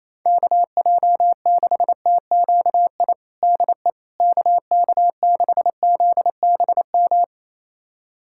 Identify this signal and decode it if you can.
Morse code, heard as a keyed tone, about 28 wpm, 700 Hz. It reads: KJ6TQI DE KK6ZBM
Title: USA Callsign DE Callsign 28wpm